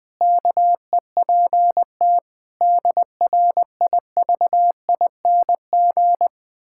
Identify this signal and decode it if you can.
Morse code, heard as a keyed tone, about 20 wpm, 700 Hz. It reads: KEPT DRIVING
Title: Pangram Challenge v2 - 20wpm